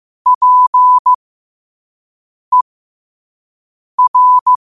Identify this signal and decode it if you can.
Morse code, heard as a keyed tone, about 15 wpm, spaced as for 5 wpm, 1000 Hz. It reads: PER
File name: qotdfiles / 5wpm